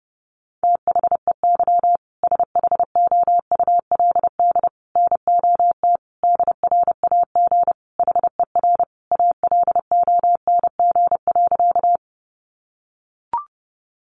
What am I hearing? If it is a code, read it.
Morse code, heard as a keyed tone, about 30 wpm, 700 Hz. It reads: THEY SHOULD NOT DRAG HER ALONG.
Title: Sentences from Top 1500 Words 30wpm